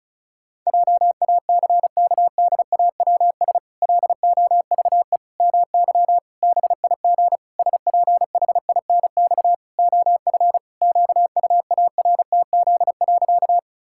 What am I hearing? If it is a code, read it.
Morse code, heard as a keyed tone, about 34 wpm, 700 Hz. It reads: JACKDAWS LOVE MY BIG SPHINX OF QUARTZ.
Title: Pangram Challenge v2 - 35wpm